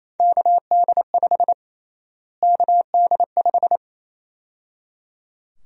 Morse code, heard as a keyed tone, about 28 wpm, 700 Hz. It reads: KD5 KD5
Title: USA Call Sign Prefixes - Repeated 2x - 28wpm